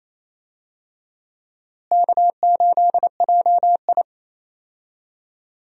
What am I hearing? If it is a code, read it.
Morse code, heard as a keyed tone, about 28 wpm, 700 Hz. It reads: K8JI
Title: USA Call Signs - 4 Characters - Repeated 3x - 28wpm